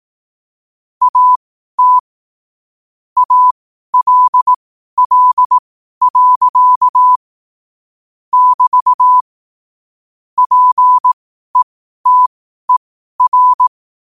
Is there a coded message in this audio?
Yes: Morse code from 1.0 to 13.7 s, heard as a keyed tone, about 18 wpm, spaced as for 12 wpm, 1000 Hz. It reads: AT ALL. <BT> PETER